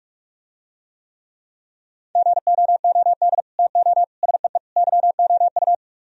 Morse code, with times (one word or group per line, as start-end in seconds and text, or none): 2.2-3.4 GOOD
3.6-4.0 TO
4.2-4.6 SEE
4.8-5.7 YOU